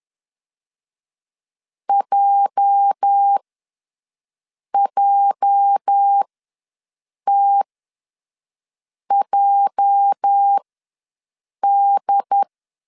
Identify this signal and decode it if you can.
Morse code, heard as a keyed tone, about 10 wpm, spaced as for 5 wpm, 800 Hz. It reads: JJTJD
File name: LEZIONI / 8 YJ45 / 8 YJ ws 10wpm 4043